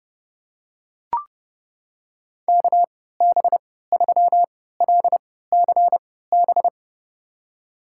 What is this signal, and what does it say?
Morse code, heard as a keyed tone, about 30 wpm, spaced as for 17 wpm, 700 Hz. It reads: KB3LCB
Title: USA Call Signs - 3x Character Spacing - 30wpm